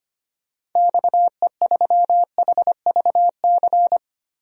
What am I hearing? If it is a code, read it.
Morse code, heard as a keyed tone, about 25 wpm, 700 Hz. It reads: XE3HVC
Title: DXCC-100 Call Signs - 6 Characters - 25wpm